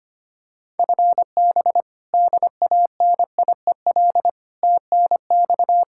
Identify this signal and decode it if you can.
Morse code, heard as a keyed tone, about 25 wpm, 700 Hz. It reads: FB DANIEL TNX
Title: Practice QSO - Exchange 3 - More Info - 25wpm